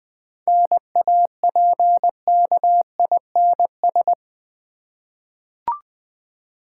Morse code, heard as a keyed tone, about 20 wpm, 700 Hz. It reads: NAPKINS
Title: Easy Words with Hints - 20wpm